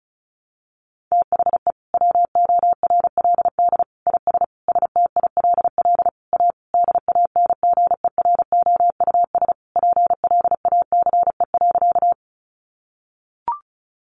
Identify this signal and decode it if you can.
Morse code, heard as a keyed tone, about 34 wpm, 700 Hz. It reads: THE WORLD IS STILL A DANGEROUS PLACE.
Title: Sentences from Top 1200 Words 35wpm